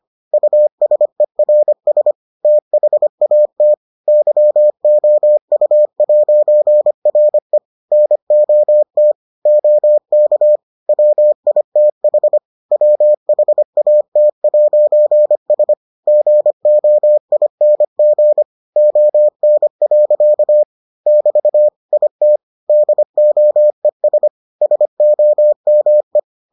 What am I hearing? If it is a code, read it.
Morse code, heard as a keyed tone, about 25 wpm, 600 Hz. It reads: USERS THAT YOU'RE NOT OK WITH WHAT'S GOING ON. <BT> IT DOES SOME